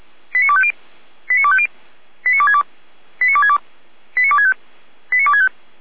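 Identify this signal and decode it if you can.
Five-tone selective call (ZVEI), heard as quick runs of five tones, tones of about 70 ms. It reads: 89250 89250 89262 89262 89265 89265
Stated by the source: ZVEI five-tone selective call 89303, received on an SDR.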